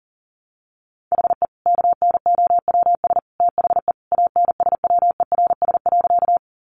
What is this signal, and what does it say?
Morse code, heard as a keyed tone, about 40 wpm, 700 Hz. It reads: HE KNOWS THE ANSWERS.
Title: Additional Easy Phrases 40wpm